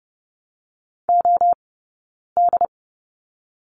Morse code, heard as a keyed tone, about 30 wpm, 700 Hz. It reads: O D
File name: Lesson 19d - Letter w - Two Characters 30wpm